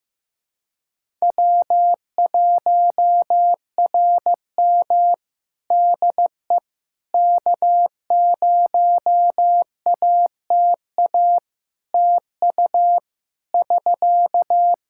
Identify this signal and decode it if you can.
Morse code, heard as a keyed tone, about 15 wpm, 700 Hz. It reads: W1RM DE K0ATA TU <SK>
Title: Practice QSO - Exchange 4 - Ending - 15wpm